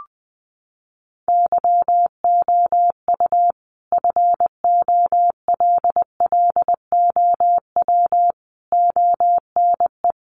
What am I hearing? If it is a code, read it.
Morse code, heard as a keyed tone, about 20 wpm, 700 Hz. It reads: YOU FOLLOW ONE
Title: Sentences from Top 1300 Words 20wpm